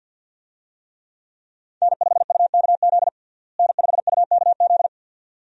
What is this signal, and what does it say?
Morse code, heard as a keyed tone, about 50 wpm, 700 Hz. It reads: N5UKZ N5UKZ
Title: USA Call Signs - 5 Characters - Repeated 2x - 50wpm